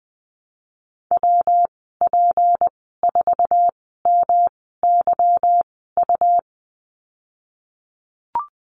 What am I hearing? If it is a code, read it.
Morse code, heard as a keyed tone, about 20 wpm, spaced as for 15 wpm, 700 Hz. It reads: WP4MYU
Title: 186 Lesson 30g - Letter b - Call Signs 20wpm